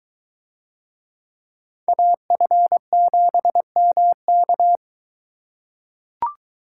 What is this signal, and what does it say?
Morse code, heard as a keyed tone, about 21 wpm, 700 Hz. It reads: AF7MK